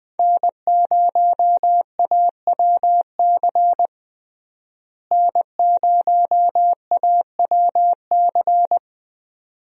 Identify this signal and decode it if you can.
Morse code, heard as a keyed tone, about 21 wpm, 700 Hz. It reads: N0AWC N0AWC
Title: USA Call Signs - 5 Characters - Repeated 2x - 20wpm